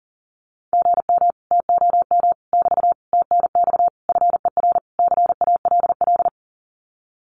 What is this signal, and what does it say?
Morse code, heard as a keyed tone, about 40 wpm, 700 Hz. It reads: GM TOM <BT> TNX FER CALL
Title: Common QSO Phrases Encore 40wpm